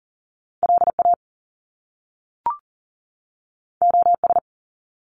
Morse code, heard as a keyed tone, about 40 wpm, 700 Hz. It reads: LA OS
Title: Top Two Letter Combinations in Words 40wpm